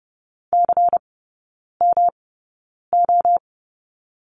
Morse code, heard as a keyed tone, about 30 wpm, spaced as for 9 wpm, 700 Hz. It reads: CMO
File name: Lesson 18e - Letter m - Three Characters 30wpm